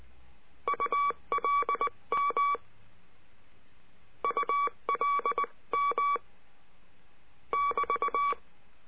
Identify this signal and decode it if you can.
Morse code, heard as a keyed tone, about 19 wpm, 1140 Hz. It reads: ULM ULM <BT>
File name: cursos / CURSO DE TELEGRAFIA / Telegrafia - Trilha 22